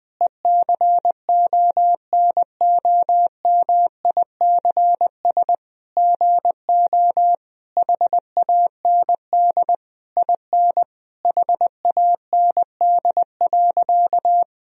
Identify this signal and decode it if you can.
Morse code, heard as a keyed tone, about 20 wpm, 700 Hz. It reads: ECONOMICS GO HAND IN HAND.
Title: Sentences from Top 1000 Words - Review Part 3 - 20wpm